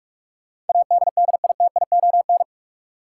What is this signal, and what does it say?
Morse code, heard as a keyed tone, about 46 wpm, 700 Hz. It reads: ADDITION
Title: Sentences from Top 1000 Words - Review Part 6 - New Word Drill - 45wpm